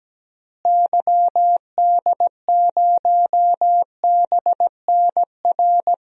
Morse code, heard as a keyed tone, about 17 wpm, 700 Hz. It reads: YD0BNR
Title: DXCC-100 Call Signs - 6 Characters - Repeated 3x - 17wpm